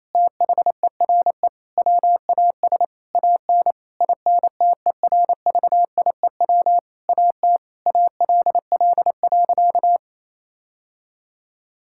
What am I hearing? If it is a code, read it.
Morse code, heard as a keyed tone, about 28 wpm, 700 Hz. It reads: THERE WAS AN INTERVIEW AT ALL.
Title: Sentences from Top 1600 Words 28wpm